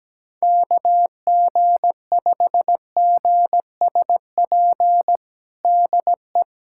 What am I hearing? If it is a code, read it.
Morse code, heard as a keyed tone, about 17 wpm, 700 Hz. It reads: KG5GSP DE
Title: USA Callsign DE Callsign 17wpm